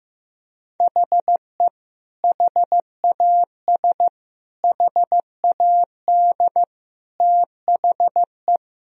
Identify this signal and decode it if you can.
Morse code, heard as a keyed tone, about 15 wpm, 700 Hz. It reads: HE HAS HAD THE
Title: Sentences from Top 1000 Words - Review Part 1 - 15wpm